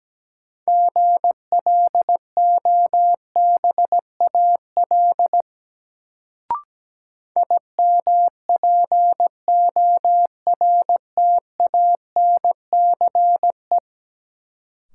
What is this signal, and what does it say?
Morse code, heard as a keyed tone, about 17 wpm, 700 Hz. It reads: GLOBAL IMPORTANCE
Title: Sentences from Top 1200 Words - New Word Drill Alt - 17wpm